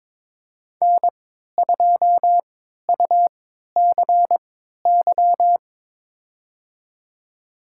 Morse code, heard as a keyed tone, about 22 wpm, spaced as for 12 wpm, 700 Hz. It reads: N2UCY
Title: USA Call Signs - 5 Characters - 3x Character Spacing - 22wpm